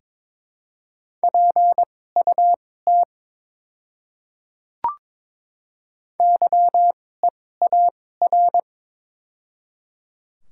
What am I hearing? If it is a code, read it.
Morse code, heard as a keyed tone, about 22 wpm, spaced as for 16 wpm, 700 Hz. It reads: PUT YEAR
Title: 2x Character Spacing - Top 200 Words - 22wpm